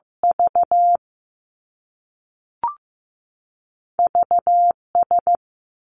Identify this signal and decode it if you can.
Morse code, heard as a keyed tone, about 15 wpm, 700 Hz. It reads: V V S
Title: Two Challenging Letter-Number Combinations 15wpm